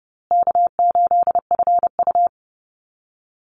Morse code, heard as a keyed tone, about 30 wpm, 700 Hz. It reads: K8FU
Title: Easy Call Signs Repeated 3x - 30wpm